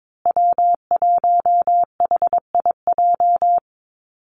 Morse code, heard as a keyed tone, about 22 wpm, 700 Hz. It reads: W1HIJ
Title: Call Signs Repeated 2x - 22wpm